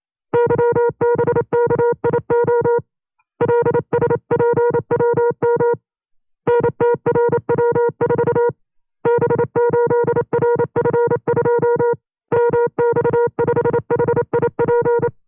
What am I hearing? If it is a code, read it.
Morse code, heard as a keyed tone, about 28 wpm, 480 Hz. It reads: YBKIO LSPWM NTRW4 B8RF2 MX5HIP
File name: FreeFiles / CW-practice / 25WPM-side1